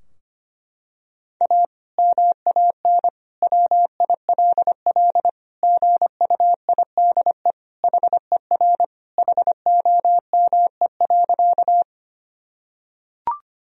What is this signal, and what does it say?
Morse code, heard as a keyed tone, about 25 wpm, 700 Hz. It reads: A MAN WILL GUIDE HER HOME.
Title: Sentences from Top 1000 Words - Review Part 3 - 25wpm